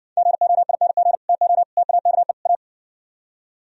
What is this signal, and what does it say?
Morse code, heard as a keyed tone, about 69 wpm, 700 Hz. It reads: GOING TO TAKE A